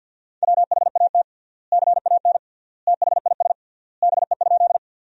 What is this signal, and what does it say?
Morse code, heard as a keyed tone, about 50 wpm, 700 Hz. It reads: WHAT CAN THIS BE?
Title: 3x Word-Spacing - Sentences from Top 100 Words - 50wpm